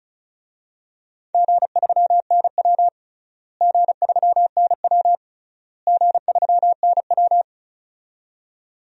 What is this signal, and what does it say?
Morse code, heard as a keyed tone, about 34 wpm, 700 Hz. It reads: G3NW G3NW G3NW
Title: DXCC-100 Call Signs - 4 Characters - Repeated 3x - 35wpm